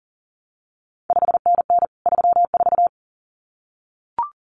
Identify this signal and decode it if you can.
Morse code, heard as a keyed tone, about 41 wpm, 700 Hz. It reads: 5NN 34
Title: CQ WW DX Contest Exchange Encore - 40wpm